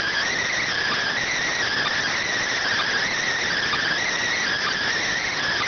Emergency Alert System (SAME) tones, heard as harsh buzzing bursts. It is heard from 1.0 to 4.8 s.